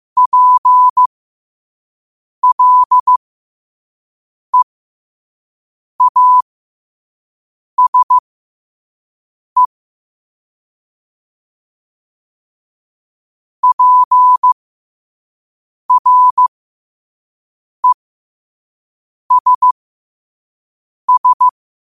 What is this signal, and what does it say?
Morse code, heard as a keyed tone, about 15 wpm, spaced as for 5 wpm, 1000 Hz. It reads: PLEASE PRESS
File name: qotdfiles / 5wpm